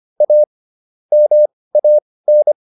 Morse code, heard as a keyed tone, about 25 wpm, spaced as for 18 wpm, 600 Hz. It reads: A MAN